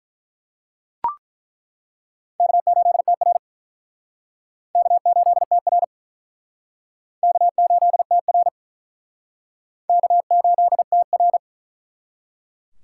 Morse code, heard as a keyed tone, about 48 wpm, 700 Hz. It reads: K8TR K8TR K8TR K8TR